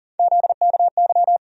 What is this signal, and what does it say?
Morse code, heard as a keyed tone, about 40 wpm, 700 Hz. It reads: ZKY
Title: USA Call Sign Postfixes - 3 Letters 40wpm